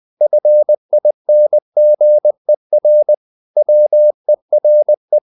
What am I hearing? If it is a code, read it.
Morse code, heard as a keyed tone, about 20 wpm, 600 Hz. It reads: FINGER WERE
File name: the old man and the sea / the old man and the sea part 1 20wpm